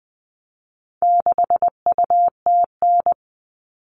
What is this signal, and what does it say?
Morse code, heard as a keyed tone, about 20 wpm, 700 Hz. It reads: 6UTN
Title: All Cut Numbers - 4-Digit 20wpm